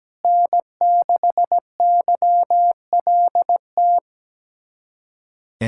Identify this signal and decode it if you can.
Morse code, heard as a keyed tone, about 17 wpm, 700 Hz. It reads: N6YLT
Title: USA Callsign DE Callsign - 5 Characters - 17wpm